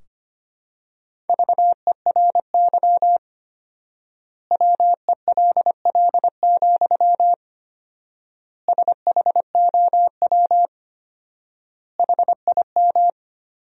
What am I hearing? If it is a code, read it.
Morse code, heard as a keyed tone, about 25 wpm, 700 Hz. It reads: VERY WELL, SHOW HIM
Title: 4x Word-Spacing - Sentences from Top 200 Words - 25wpm